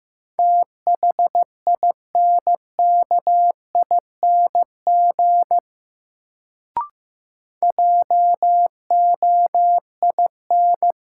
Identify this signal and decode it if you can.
Morse code, heard as a keyed tone, about 15 wpm, 700 Hz. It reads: THINKING JOIN
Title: Sentences from Top 1000 Words - Review Part 6 - New Word Drill - 15wpm